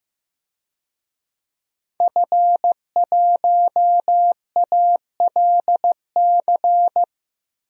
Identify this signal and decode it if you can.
Morse code, heard as a keyed tone, about 15 wpm, 700 Hz. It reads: F1ALC
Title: DXCC-100 Call Signs - 5 Characters - Repeated 3x - 15wpm